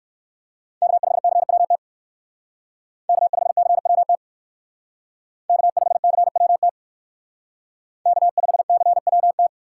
Morse code, heard as a keyed tone, about 63 wpm, 700 Hz. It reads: K5CWT K5CWT K5CWT K5CWT